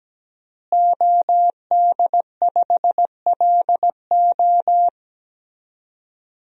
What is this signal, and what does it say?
Morse code, heard as a keyed tone, about 17 wpm, 700 Hz. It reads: OD5LO
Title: DXCC-100 Call Signs - 5 Characters - Repeated 3x - 17wpm